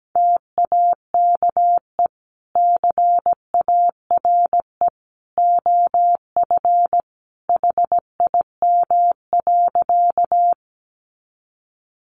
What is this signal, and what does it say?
Morse code, heard as a keyed tone, about 17 wpm, 700 Hz. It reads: TAKE CARE OF HIM.
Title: Sentences from Top 300 Words 17wpm